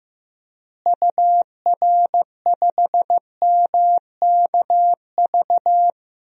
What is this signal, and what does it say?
Morse code, heard as a keyed tone, about 15 wpm, 700 Hz. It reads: UR5MKV
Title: DXCC-100 Call Signs - 6 Characters - Repeated 2x - 15wpm